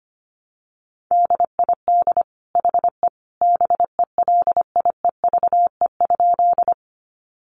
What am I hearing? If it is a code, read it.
Morse code, heard as a keyed tone, about 25 wpm, 700 Hz. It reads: DID HE BELIEVE?
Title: Sentences from Top 1000 Words 25wpm